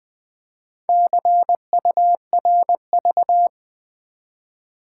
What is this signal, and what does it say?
Morse code, heard as a keyed tone, about 20 wpm, 700 Hz. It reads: CURV
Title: Build A Word - 1 Letter At A Time - 5-Letter Words - 20wpm